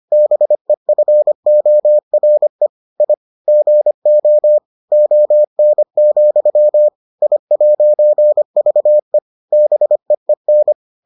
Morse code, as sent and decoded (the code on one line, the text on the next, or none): -... ..-. --- .-. / .. / --. --- / --- -. --..-- / .. .----. ...- . / -... . -.
BEFORE I GO ON, I'VE BEEN